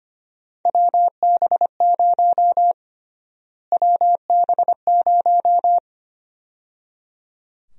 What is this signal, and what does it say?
Morse code, heard as a keyed tone, about 25 wpm, 700 Hz. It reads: WB0 WB0
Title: USA Call Sign Prefixes - Repeated 2x - 25wpm